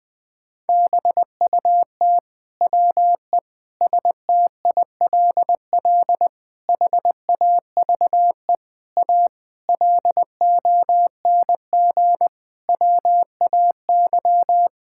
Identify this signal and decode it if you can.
Morse code, heard as a keyed tone, about 21 wpm, 700 Hz. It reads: BUT WE STILL HAVE A LONG WAY
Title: Sets of 9 Words - 20wpm